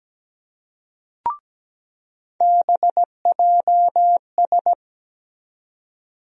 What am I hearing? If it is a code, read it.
Morse code, heard as a keyed tone, about 17 wpm, 700 Hz. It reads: BJS
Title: USA Call Sign Postfixes - 3 Letters 17wpm